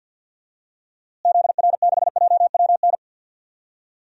Morse code, heard as a keyed tone, about 51 wpm, 700 Hz. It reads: ZR6JWN
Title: DXCC-100 Call Signs - 6 Characters - Repeated 2x - 50wpm